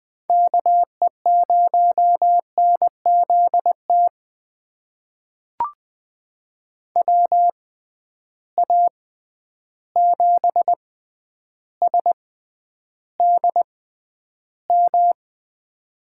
Morse code, heard as a keyed tone, about 20 wpm, spaced as for 10 wpm, 700 Hz. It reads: KE0NZT WA7SDM
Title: USA Call Signs - 6 Characters - 6x Character Spacing - 20wpm